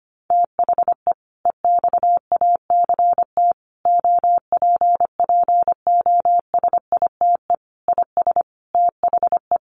Morse code, heard as a keyed tone, about 25 wpm, 700 Hz. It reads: THE EXACT OPPOSITE IS THE
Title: Sentences from Top 1300 Words 25wpm